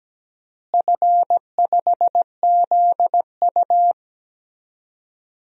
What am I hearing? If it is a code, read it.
Morse code, heard as a keyed tone, about 17 wpm, 700 Hz. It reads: F5ZU
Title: DXCC-100 Call Signs - 4 Characters - Repeated 3x - 17wpm